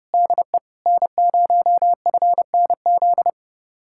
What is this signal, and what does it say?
Morse code, heard as a keyed tone, about 36 wpm, 700 Hz. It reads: DE N0FNZ
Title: Speed-Racing Common QSO Phrases Encore - 30wpm